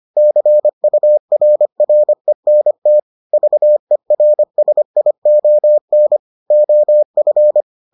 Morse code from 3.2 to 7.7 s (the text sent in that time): VERSION OF